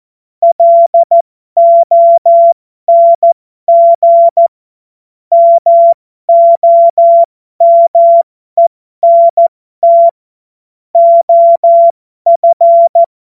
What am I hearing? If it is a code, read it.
Morse code, heard as a keyed tone, about 14 wpm, spaced as for 12 wpm, 680 Hz. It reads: LONG MOMENT OF